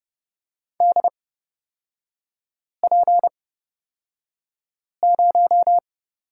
Morse code, heard as a keyed tone, about 30 wpm, spaced as for 5 wpm, 690 Hz. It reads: DP0
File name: Lesson 37e - Letter x - Three Characters 30wpm